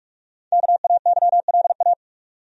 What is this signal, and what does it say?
Morse code, heard as a keyed tone, about 42 wpm, 700 Hz. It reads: KAYLA
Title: USA Common Names - 45wpm